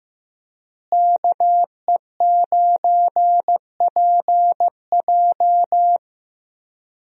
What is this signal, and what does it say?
Morse code, heard as a keyed tone, about 15 wpm, 700 Hz. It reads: KE9PJ
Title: USA Call Signs - 5 Characters - Repeated 2x - 15wpm